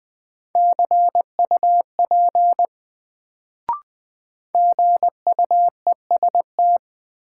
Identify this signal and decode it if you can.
Morse code, heard as a keyed tone, about 20 wpm, 700 Hz. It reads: CUP GUEST
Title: Sentences from Top 1600 Words - New Word Drill - 20wpm